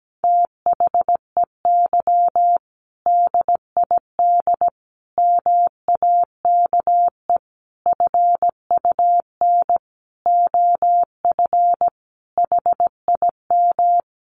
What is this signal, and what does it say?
Morse code, heard as a keyed tone, about 17 wpm, 700 Hz. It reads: THEY DID MAKE FUN OF HIM
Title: Sentences from Top 1300 Words 17wpm